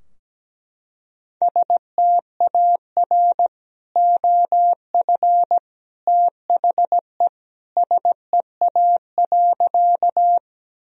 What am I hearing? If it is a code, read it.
Morse code, heard as a keyed tone, about 17 wpm, 700 Hz. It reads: STAR OF THE SEA.
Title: Sentences from Top 1000 Words - Review Part 2 - 17wpm